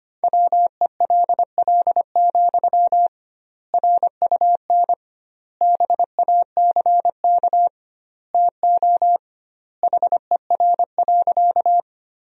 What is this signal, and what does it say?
Morse code, heard as a keyed tone, about 25 wpm, 700 Hz. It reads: WELL, RUN BACK TO HER.
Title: 2x Word-Spacing - Sentences from Top 300 Words - 25wpm